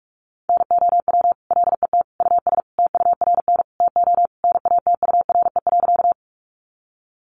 Morse code, heard as a keyed tone, about 44 wpm, 700 Hz. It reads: NOW LET US TURN TO NATURE.
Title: Sentences from Top 600 Words 45wpm